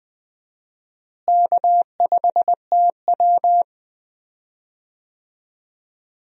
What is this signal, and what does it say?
Morse code, heard as a keyed tone, about 20 wpm, 700 Hz. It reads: K5TW
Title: USA Call Signs - 4 Characters - Repeated 3x - 20wpm